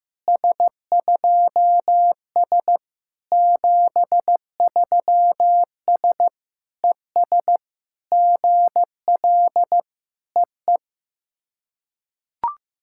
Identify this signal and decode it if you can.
Morse code, heard as a keyed tone, about 14 wpm, 700 Hz. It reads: S2S 73S ES GL EE
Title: SOTA QSO Practice 15wpm